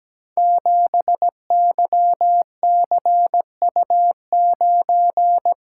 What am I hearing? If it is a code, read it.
Morse code, heard as a keyed tone, about 17 wpm, 700 Hz. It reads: 7YCU9
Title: Code Groups - 5 Letter:Numbers - 17wpm